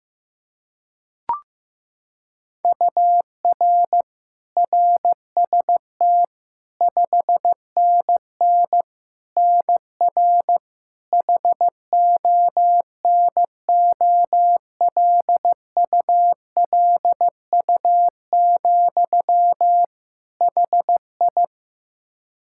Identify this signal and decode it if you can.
Morse code, heard as a keyed tone, about 15 wpm, 700 Hz. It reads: UR RST 5NN NR HONOLULU, HI
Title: Practice QSO - Exchange 1 - Beginning - 15wpm